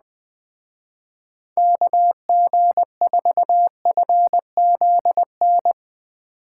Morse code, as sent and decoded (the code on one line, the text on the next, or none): -.- --. ....- ..-. --.. -.
KG4FZN